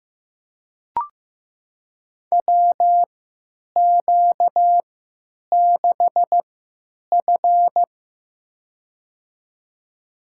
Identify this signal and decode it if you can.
Morse code, heard as a keyed tone, about 15 wpm, spaced as for 9 wpm, 700 Hz. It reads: WQ6F
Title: USA Call Signs - 4 Characters - 3x Character Spacing - 15wpm